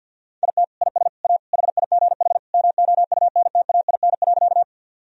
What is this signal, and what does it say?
Morse code, heard as a keyed tone, about 50 wpm, 700 Hz. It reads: IT IS A HIGH MOUNTAIN.
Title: Sentences from Top 1000 Words - Review Part 1 - 50wpm